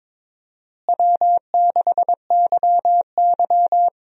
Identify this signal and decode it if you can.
Morse code, heard as a keyed tone, about 21 wpm, 700 Hz. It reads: W6YY
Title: USA Call Signs - 4 Characters - Repeated 2x - 22wpm